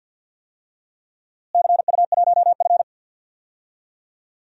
Morse code, heard as a keyed tone, about 52 wpm, 700 Hz. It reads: CU1F
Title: DXCC-100 Call Signs - 4 Characters - Repeated 2x - 50wpm